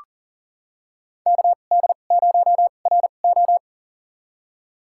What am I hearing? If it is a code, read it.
Morse code, heard as a keyed tone, about 38 wpm, spaced as for 28 wpm, 700 Hz. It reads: KD0RO
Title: USA Call Signs - 5 Characters - 2x Character Spacing - 40wpm